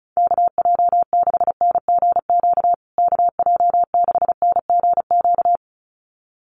Morse code, heard as a keyed tone, about 34 wpm, 700 Hz. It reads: KJ6NGQ KJ6NGQ
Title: USA Call Signs Repeated 2x 35wpm